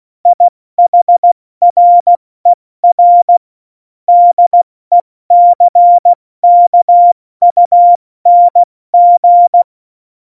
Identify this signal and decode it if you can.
Morse code, heard as a keyed tone, about 16 wpm, 700 Hz. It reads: IHRER DECKUNG
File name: Morsekurs / Witze / Witz 13. Oktober 2020, 70 F80